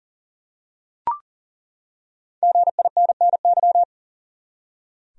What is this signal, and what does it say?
Morse code, heard as a keyed tone, about 40 wpm, 700 Hz. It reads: GINNY
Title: USA Common Names - 40wpm